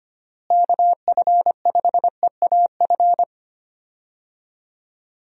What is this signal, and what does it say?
Morse code, heard as a keyed tone, about 25 wpm, 700 Hz. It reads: KF5EAF
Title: USA Call Signs - 6 Characters - Repeated 2x - 25wpm